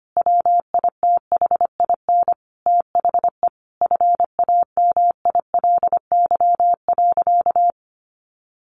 Morse code, heard as a keyed tone, about 25 wpm, 700 Hz. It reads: WITHIN THE FAMILY.
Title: Sentences from Top 1500 Words 25wpm